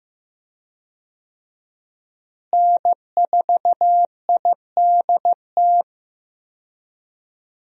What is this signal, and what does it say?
Morse code, heard as a keyed tone, about 15 wpm, 700 Hz. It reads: N4IDT
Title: USA Call Signs - 5 Characters - Repeated 3x - 15wpm